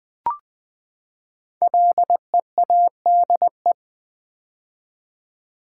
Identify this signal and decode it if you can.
Morse code, heard as a keyed tone, about 20 wpm, 700 Hz. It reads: LEADE
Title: Build A Word - 5 Letters At A Time - 10-Letter Words - 20wpm